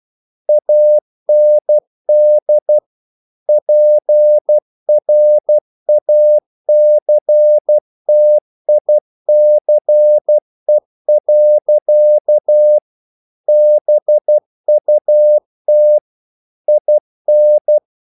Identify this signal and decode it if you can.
Morse code, heard as a keyed tone, about 12 wpm, 600 Hz. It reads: AND PRACTICE. BUT IN
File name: CW / Practice / EnglishQuotations0112